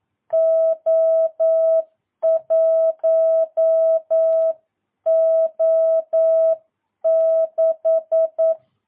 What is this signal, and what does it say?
Morse code, heard as a keyed tone, about 9 wpm, 650 Hz. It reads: O1O6
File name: jh8cbh / 2022cw